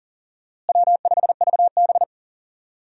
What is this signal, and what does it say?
Morse code, heard as a keyed tone, about 40 wpm, 700 Hz. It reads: W5VB
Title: USA Call Signs - 4 Characters - Repeated 2x - 40wpm